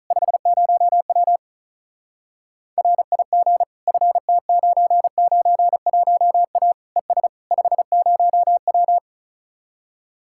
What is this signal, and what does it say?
Morse code, heard as a keyed tone, about 44 wpm, 700 Hz. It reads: 50W RIG FT991A ES 50W